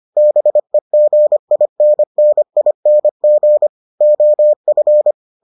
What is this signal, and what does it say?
Morse code, heard as a keyed tone, about 25 wpm, 600 Hz. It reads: BEGINNING OF